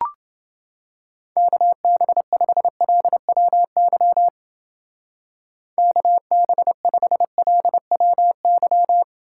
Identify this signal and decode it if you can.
Morse code, heard as a keyed tone, about 29 wpm, 700 Hz. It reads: KB5LWY KB5LWY